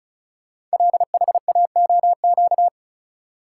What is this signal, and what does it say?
Morse code, heard as a keyed tone, about 36 wpm, 700 Hz. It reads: LHAOQ
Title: Code Groups - 5 Letters - 35wpm